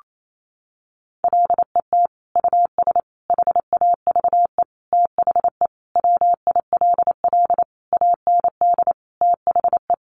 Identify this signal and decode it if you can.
Morse code, heard as a keyed tone, about 28 wpm, 700 Hz. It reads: LET US HAVE THE WILL AND THE
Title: Sentences from Top 1300 Words 28wpm